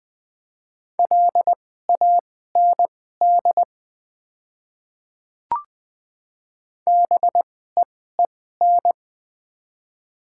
Morse code, heard as a keyed tone, about 20 wpm, spaced as for 14 wpm, 700 Hz. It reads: LAND BEEN
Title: 2x Character Spacing - Top 200 Words - 20wpm